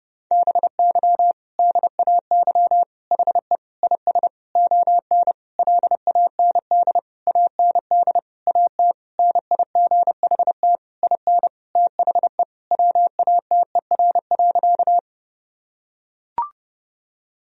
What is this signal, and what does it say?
Morse code, heard as a keyed tone, about 31 wpm, 700 Hz. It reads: BY DAY HE IS ON LAND AND AT NIGHT IN THE WATER.
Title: Sentences from Top 1000 Words - Review Part 1 - 30wpm